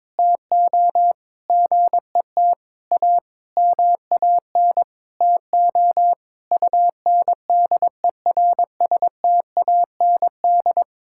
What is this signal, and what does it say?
Morse code, heard as a keyed tone, about 22 wpm, 700 Hz. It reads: TO GET A MAN TO UNDERSTAND
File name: Sentences with 9-Word Phrases - 22wpm